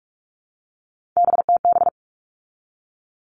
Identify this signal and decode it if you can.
Morse code, heard as a keyed tone, about 44 wpm, 700 Hz. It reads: BTB
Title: All Cut Numbers - 3-Digit 45wpm